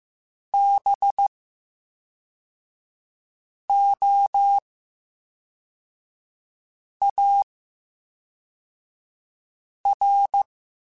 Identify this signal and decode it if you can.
Morse code, heard as a keyed tone, about 15 wpm, 800 Hz. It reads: B O A R
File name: Special INGs / Special INGs-mp3s / VerySlowToFaster Ing @10